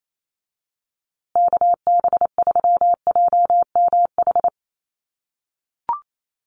Morse code, heard as a keyed tone, about 28 wpm, 700 Hz. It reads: KB3JMH
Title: USA Call Signs Repeated 3x 28wpm